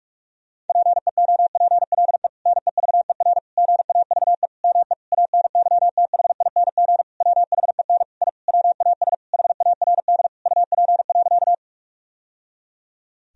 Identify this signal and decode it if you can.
Morse code, heard as a keyed tone, about 46 wpm, 700 Hz. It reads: PEOPLE NEVER GAVE ME ANYTHING WHEN I WAS HARD UP.
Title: Sentences from Top 1000 Words - Review Part 1 - 45wpm